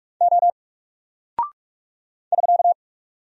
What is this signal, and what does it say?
Morse code, heard as a keyed tone, about 44 wpm, 700 Hz. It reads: O <SK>
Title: Single Character (Letter:Number:Punctuation:Prosign) Alt 45wpm